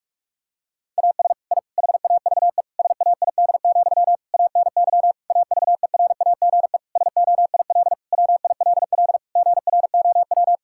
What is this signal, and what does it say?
Morse code, heard as a keyed tone, about 46 wpm, 700 Hz. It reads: AS I HAVE SAID, ANY AVERAGE SOIL WILL GROW
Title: Sentences from Top 1000 Words - Review Part 6 - 45wpm